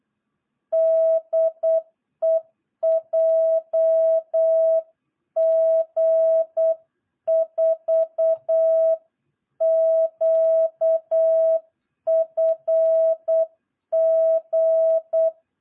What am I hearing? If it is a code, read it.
Morse code, heard as a keyed tone, about 8 wpm, 660 Hz. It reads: DE JG4QFG
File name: jh8cbh / 2022cw